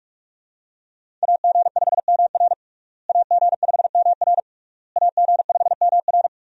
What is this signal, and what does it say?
Morse code, heard as a keyed tone, about 46 wpm, 700 Hz. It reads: AG5MR AG5MR AG5MR
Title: USA Call Signs - 5 Characters - Repeated 3x - 45wpm